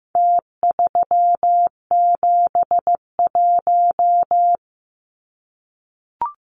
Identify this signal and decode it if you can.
Morse code, heard as a keyed tone, about 15 wpm, 700 Hz. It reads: T371
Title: Numbers - Contest Style - 4-Digit 15wpm